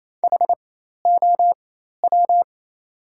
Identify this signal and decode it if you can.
Morse code, heard as a keyed tone, about 27 wpm, spaced as for 13 wpm, 700 Hz. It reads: HOW
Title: 4x Character Spacing - Top 100 Words - 28wpm